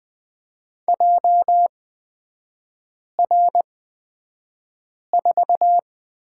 Morse code, heard as a keyed tone, about 20 wpm, spaced as for 5 wpm, 700 Hz. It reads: JR4
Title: 236 Lesson 38e - Letter q - Three Characters 20wpm